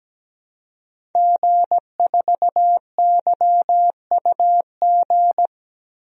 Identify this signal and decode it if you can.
Morse code, heard as a keyed tone, about 17 wpm, 700 Hz. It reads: G4YUG
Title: DXCC-100 Call Signs - 5 Characters - Repeated 3x - 17wpm